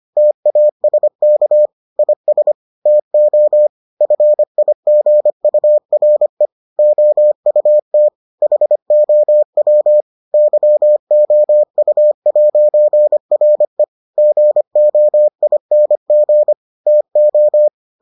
Morse code, heard as a keyed tone, about 25 wpm, 600 Hz. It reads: TASK IS TO FIGURE OUT HOW YOU'RE GOING TO